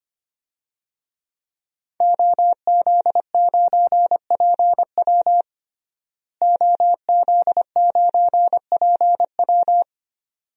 Morse code, heard as a keyed tone, about 25 wpm, 700 Hz. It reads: OZ9PW OZ9PW
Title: DXCC-100 Call Signs - 5 Characters - Repeated 2x - 25wpm